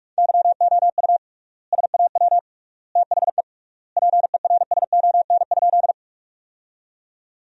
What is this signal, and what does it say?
Morse code, heard as a keyed tone, about 41 wpm, 700 Hz. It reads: YOU SAW THE PERSON?
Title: 3x Word-Spacing - Sentences from Top 400 Words - 45wpm